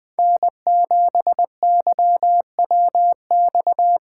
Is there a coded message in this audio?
Yes: Morse code from 0.2 to 4.0 s, heard as a keyed tone, about 20 wpm, 700 Hz. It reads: N7YWX